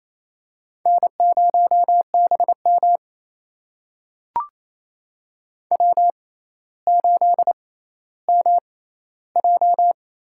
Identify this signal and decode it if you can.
Morse code, heard as a keyed tone, about 28 wpm, spaced as for 14 wpm, 700 Hz. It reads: N0BM W8MJ
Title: USA Call Signs - 6x Character Spacing - 28wpm